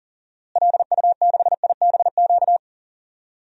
Morse code, heard as a keyed tone, about 40 wpm, 700 Hz. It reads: LU6IBQ
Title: DXCC-100 Call Signs - 6 Characters - Repeated 2x - 40wpm